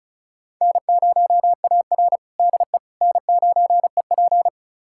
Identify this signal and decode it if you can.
Morse code, heard as a keyed tone, about 36 wpm, 700 Hz. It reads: N0AR DE N9EP
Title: USA Callsign DE Callsign - 4 Characters - 35wpm